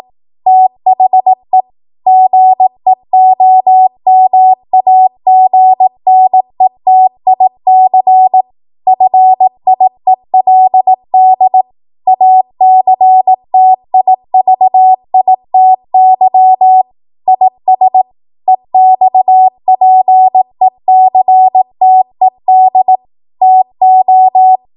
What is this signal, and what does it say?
Morse code, heard as a keyed tone, about 18 wpm, 750 Hz. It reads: THE GEOMAGNETIC FIELD ACTIVITY IS EXPECTED TO
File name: Morse / 18WPM